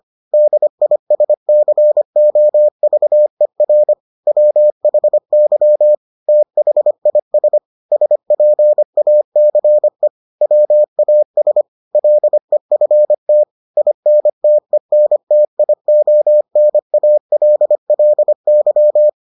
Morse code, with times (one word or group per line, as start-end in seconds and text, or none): 0.3-3.9 DISCOVER
4.3-6.0 WHY
6.3-7.6 THIS
7.9-10.1 SPACE
10.4-11.6 WAS
11.9-13.4 LEFT
13.8-19.1 INTENTIONALLY